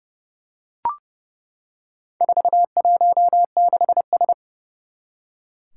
Morse code, heard as a keyed tone, about 29 wpm, 700 Hz. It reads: 416S
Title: Four Challenging Letter-Number Combinations 30wpm